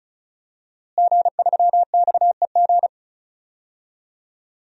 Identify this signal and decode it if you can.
Morse code, heard as a keyed tone, about 34 wpm, 700 Hz. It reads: G3XEG
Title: DXCC-100 Call Signs - 5 Characters - Repeated 3x - 35wpm